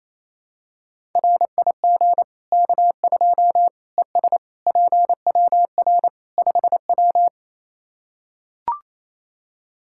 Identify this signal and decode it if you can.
Morse code, heard as a keyed tone, about 27 wpm, 700 Hz. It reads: RIG K2 ES PWR 5W
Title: Practice QSO - Exchange 2 - Info - 28wpm